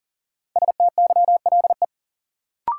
Morse code, heard as a keyed tone, about 40 wpm, 700 Hz. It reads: STYLE
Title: Sentences from Top 800 Words - New Word Drill - 40wpm